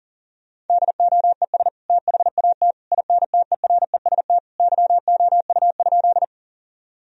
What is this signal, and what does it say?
Morse code, heard as a keyed tone, about 39 wpm, 700 Hz. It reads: DOES THAT INTEREST YOU?
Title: Sentences from Top 1000 Words - Review Part 2 - 40wpm